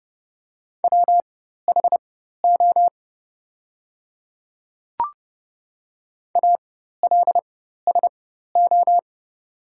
Morse code, heard as a keyed tone, about 29 wpm, spaced as for 14 wpm, 700 Hz. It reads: WHO ALSO
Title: 4x Character Spacing - Top 300 Words - 30wpm